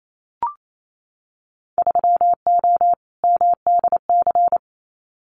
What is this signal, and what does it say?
Morse code, heard as a keyed tone, about 28 wpm, 700 Hz. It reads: 3O MDC
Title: Winter Field Day Exchanges Encore 28wpm